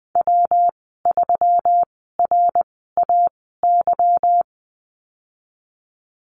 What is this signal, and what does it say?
Morse code, heard as a keyed tone, about 20 wpm, spaced as for 15 wpm, 700 Hz. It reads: W3RAY
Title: 167 Lesson 27g - Number 7 - Call Signs 20wpm